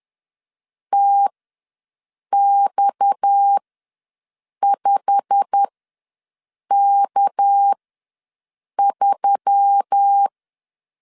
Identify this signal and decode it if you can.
Morse code, heard as a keyed tone, about 10 wpm, spaced as for 6 wpm, 800 Hz. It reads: TX5K3
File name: LEZIONI / 9 286 ws 10wpm 4032